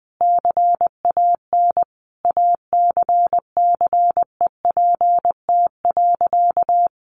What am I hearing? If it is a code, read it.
Morse code, heard as a keyed tone, about 20 wpm, 700 Hz. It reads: CAN ACCEPT.
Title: Sentences from Top 1300 Words 20wpm